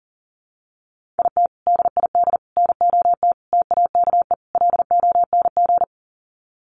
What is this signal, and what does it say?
Morse code, heard as a keyed tone, about 40 wpm, 700 Hz. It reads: IT DID NOT TAKE LONG
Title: Sets of 5 Words Encore2 40wpm